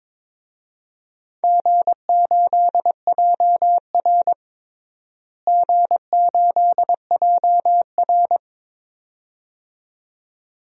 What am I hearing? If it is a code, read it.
Morse code, heard as a keyed tone, about 22 wpm, 700 Hz. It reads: G8JR G8JR
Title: DXCC-100 Call Signs - 4 Characters - Repeated 2x - 22wpm